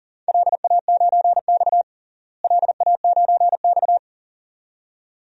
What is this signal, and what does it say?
Morse code, heard as a keyed tone, about 37 wpm, 700 Hz. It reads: LA9X LA9X
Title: DXCC-100 Call Signs - 4 Characters - Repeated 2x - 40wpm